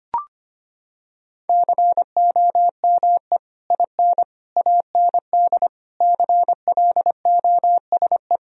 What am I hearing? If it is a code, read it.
Morse code, heard as a keyed tone, about 25 wpm, 700 Hz. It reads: COME IN AND CLOSE
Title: Sentences from Top 1000 Words - Review Part 1 - 25wpm